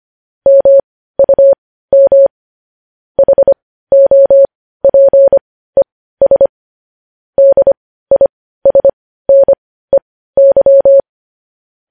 Morse code, heard as a keyed tone, about 25 wpm, spaced as for 15 wpm, 550 Hz. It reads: MUM HOPES DISNEY
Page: news-headlines-in-morse-code-at-15-wpm